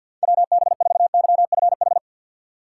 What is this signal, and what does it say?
Morse code, heard as a keyed tone, about 52 wpm, 700 Hz. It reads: WB4YFH
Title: USA Call Signs - 2x Character Spacing - 50wpm